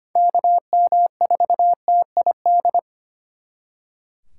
Morse code, heard as a keyed tone, about 32 wpm, 700 Hz. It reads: KM4TID